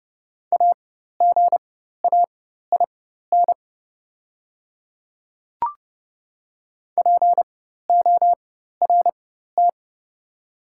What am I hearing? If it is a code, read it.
Morse code, heard as a keyed tone, about 30 wpm, spaced as for 14 wpm, 700 Hz. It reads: AGAIN PORT
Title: 4x Character Spacing - Top 300 Words - 30wpm